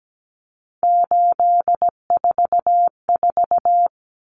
Morse code, heard as a keyed tone, about 17 wpm, 700 Hz. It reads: 844
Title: Numbers - 3-Digit 17wpm